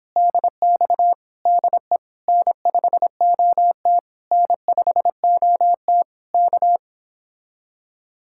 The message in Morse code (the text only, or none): DX DE N5OT N5OT K